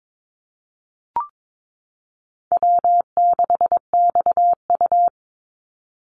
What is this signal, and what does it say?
Morse code, heard as a keyed tone, about 22 wpm, 690 Hz. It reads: W6XU
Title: Call Signs Repeated 3x - 22wpm